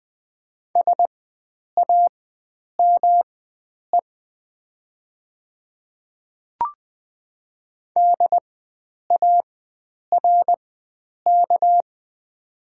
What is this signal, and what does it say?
Morse code, heard as a keyed tone, about 20 wpm, spaced as for 9 wpm, 700 Hz. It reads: SAME DARK
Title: 4x Character Spacing - Top 500 Words - 20wpm